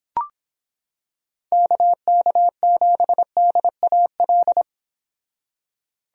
Morse code, heard as a keyed tone, about 26 wpm, 700 Hz. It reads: KK7DAL